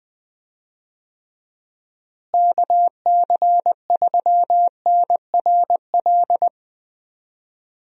Morse code, heard as a keyed tone, about 20 wpm, 700 Hz. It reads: KC3NRL
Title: USA Call Signs - 6 Characters - Repeated 3x - 20wpm